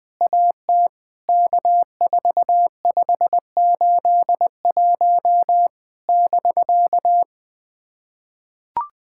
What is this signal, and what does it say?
Morse code, heard as a keyed tone, about 20 wpm, 700 Hz. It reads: AT K4581 <BK>
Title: POTA QSO Practice - 20wpm